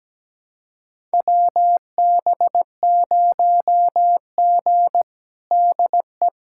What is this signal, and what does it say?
Morse code, heard as a keyed tone, about 17 wpm, 700 Hz. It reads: WB0G DE
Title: USA Callsign DE Callsign - 4 Characters - 17wpm